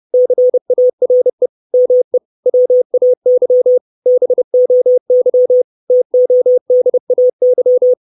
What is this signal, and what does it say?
Morse code, heard as a keyed tone, about 30 wpm, 500 Hz. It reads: CARE ME WAY BOY TODAY
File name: 632 words / 632 Words 30WPM